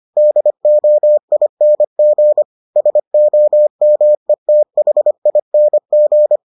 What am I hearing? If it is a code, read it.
Morse code, heard as a keyed tone, about 25 wpm, 600 Hz. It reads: DOING SOMETHING